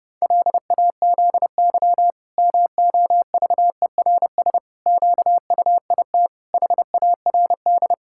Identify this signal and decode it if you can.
Morse code, heard as a keyed tone, about 30 wpm, 700 Hz. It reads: LAZY MOVERS QUIT HARD
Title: Pangram Challenge v2 - 30wpm